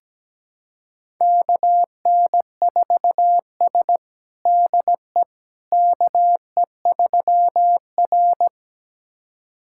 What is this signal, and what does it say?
Morse code, heard as a keyed tone, about 17 wpm, 700 Hz. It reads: KN4S DE KE3R
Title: USA Callsign DE Callsign - 4 Characters - 17wpm